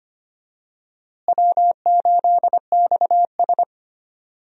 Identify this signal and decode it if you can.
Morse code, heard as a keyed tone, about 25 wpm, 700 Hz. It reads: W8XS
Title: USA Call Signs - 4 Characters - Repeated 3x - 25wpm